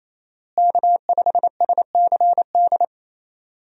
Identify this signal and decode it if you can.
Morse code, heard as a keyed tone, about 28 wpm, 700 Hz. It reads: K5SCD
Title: USA Call Signs - 5 Characters - Repeated 2x - 28wpm